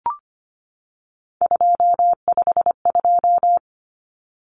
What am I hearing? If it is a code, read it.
Morse code, heard as a keyed tone, about 25 wpm, 700 Hz. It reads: 252
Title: Numbers - 3-Digit 25wpm